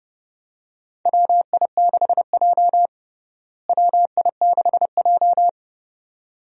Morse code, heard as a keyed tone, about 30 wpm, 700 Hz. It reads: WI6J WI6J
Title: USA Call Signs - 4 Characters - Repeated 2x - 30wpm